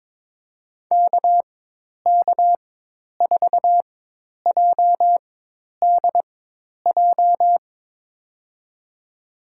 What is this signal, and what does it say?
Morse code, heard as a keyed tone, about 22 wpm, spaced as for 10 wpm, 700 Hz. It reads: KK4JDJ
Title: USA Call Signs - 4x Character Spacing - 22wpm